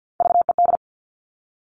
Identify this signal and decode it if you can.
Morse code, heard as a keyed tone, about 50 wpm, 700 Hz. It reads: VED
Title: Top Three Letter Combinations in Words 50wpm